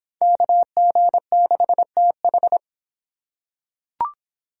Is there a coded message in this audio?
Yes: Morse code from 0.2 to 2.6 s, heard as a keyed tone, about 26 wpm, 700 Hz. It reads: KG6TH